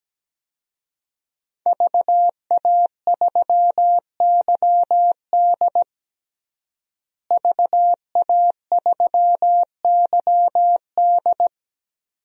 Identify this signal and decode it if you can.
Morse code, heard as a keyed tone, about 17 wpm, 700 Hz. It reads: VA3YD VA3YD
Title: DXCC-100 Call Signs - 5 Characters - Repeated 2x - 17wpm